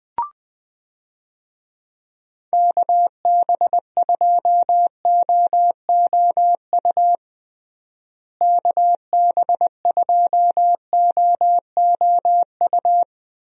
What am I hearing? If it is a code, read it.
Morse code, heard as a keyed tone, about 20 wpm, 700 Hz. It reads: KB2OOU KB2OOU
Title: USA Call Signs - 6 Characters - Repeated 2x - 20wpm